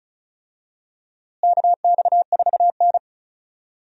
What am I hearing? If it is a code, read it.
Morse code, heard as a keyed tone, about 34 wpm, 700 Hz. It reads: KX4N
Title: USA Call Signs - 4 Characters - Repeated 2x - 35wpm